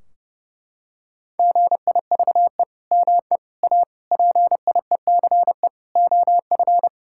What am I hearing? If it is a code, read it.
Morse code, heard as a keyed tone, about 30 wpm, 700 Hz. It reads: GIVE ME A PIECE OF
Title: Sentences from Top 1600 Words 30wpm